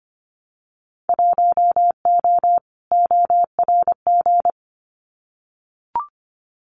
Morse code, heard as a keyed tone, about 25 wpm, 700 Hz. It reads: 1O ORG
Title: Winter Field Day Exchanges 25wpm